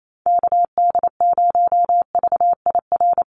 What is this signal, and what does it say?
Morse code, heard as a keyed tone, about 28 wpm, 700 Hz. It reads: KD0VIR
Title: USA Call Signs Repeated 3x 28wpm